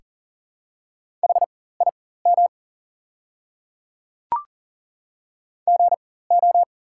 Morse code, heard as a keyed tone, about 40 wpm, spaced as for 19 wpm, 700 Hz. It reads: HIM GO
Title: 4x Character Spacing - Top 100 Words - 40wpm